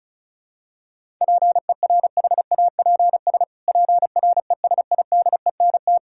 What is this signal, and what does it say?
Morse code, heard as a keyed tone, about 36 wpm, 700 Hz. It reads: PERHAPS PRESIDENT
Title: Pangram Challenge v2 - 35wpm